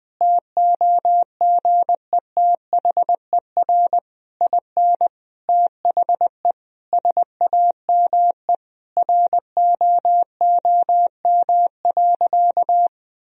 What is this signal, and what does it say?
Morse code, heard as a keyed tone, about 20 wpm, 700 Hz. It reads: TOGETHER IN THE SAME ROOM.
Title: Sentences from Top 1000 Words - Review Part 3 - 20wpm